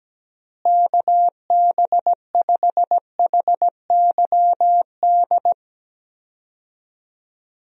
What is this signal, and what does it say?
Morse code, heard as a keyed tone, about 17 wpm, 700 Hz. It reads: KB5HYD
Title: USA Call Signs - 6 Characters - Repeated 2x - 17wpm